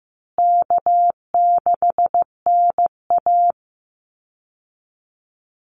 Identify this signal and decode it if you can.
Morse code, heard as a keyed tone, about 15 wpm, 700 Hz. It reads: K6NA
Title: Easy Call Signs Repeated 3x - 15wpm